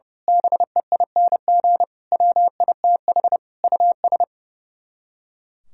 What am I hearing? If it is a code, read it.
Morse code, heard as a keyed tone, about 30 wpm, 700 Hz. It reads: BEING WITH US
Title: Speed-Racing - Sets of 5 Words 30wpm